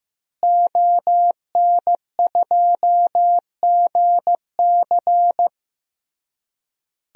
Morse code, heard as a keyed tone, about 15 wpm, 700 Hz. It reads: ON2GC
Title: DXCC-100 Call Signs - 5 Characters - Repeated 3x - 15wpm